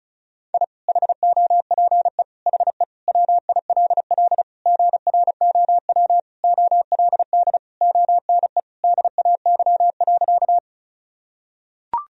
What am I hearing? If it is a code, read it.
Morse code, heard as a keyed tone, about 34 wpm, 700 Hz. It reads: I HOPE HE WILL GROW OLD ONE DAY.
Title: Sentences from Top 1000 Words - Review Part 1 - 35wpm